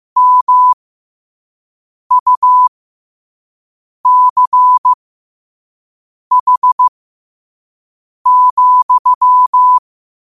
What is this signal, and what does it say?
Morse code, heard as a keyed tone, about 15 wpm, spaced as for 5 wpm, 1000 Hz. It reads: MUCH,
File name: qotdfiles / 5wpm